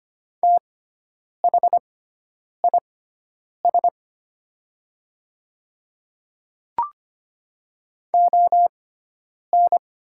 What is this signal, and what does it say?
Morse code, heard as a keyed tone, about 25 wpm, spaced as for 9 wpm, 690 Hz. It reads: THIS ON
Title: 6x Character Spacing - Top 100 Words - 25wpm